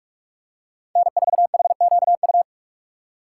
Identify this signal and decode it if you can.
Morse code, heard as a keyed tone, about 44 wpm, 700 Hz. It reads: N4HQU
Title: USA Call Signs - 5 Characters - 3x Character Spacing - 45wpm